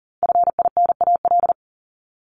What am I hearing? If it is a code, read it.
Morse code, heard as a keyed tone, about 40 wpm, 700 Hz. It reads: FINAL
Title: Top 500 Words Repeated 40wpm